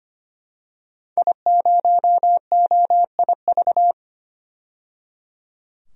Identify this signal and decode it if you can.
Morse code, heard as a keyed tone, about 25 wpm, 700 Hz. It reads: I0OIV
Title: DXCC-100 Call Signs - 5 Characters - 25wpm